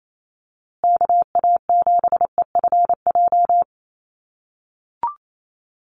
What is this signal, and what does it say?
Morse code, heard as a keyed tone, about 28 wpm, 700 Hz. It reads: KA7EFJ
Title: USA Call Signs Repeated 2x 28wpm